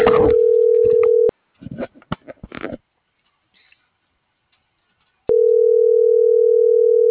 Ringback tone (North America), 440 + 480 Hz.